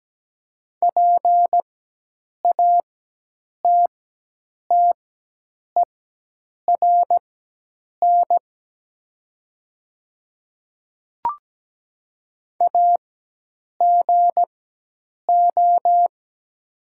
Morse code, heard as a keyed tone, about 17 wpm, spaced as for 8 wpm, 700 Hz. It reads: PATTERN AGO
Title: 4x Character Spacing - Top 500 Words - 17wpm